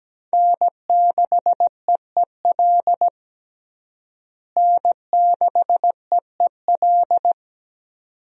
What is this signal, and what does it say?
Morse code, heard as a keyed tone, about 17 wpm, 700 Hz. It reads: N6EEL N6EEL
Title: USA Call Signs - 5 Characters - Repeated 2x - 17wpm